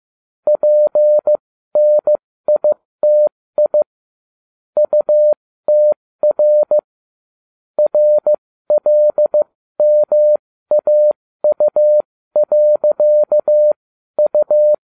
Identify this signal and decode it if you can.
Morse code, heard as a keyed tone, about 15 wpm, spaced as for 13 wpm, 600 Hz. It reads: PNITI UTR RLMAU.U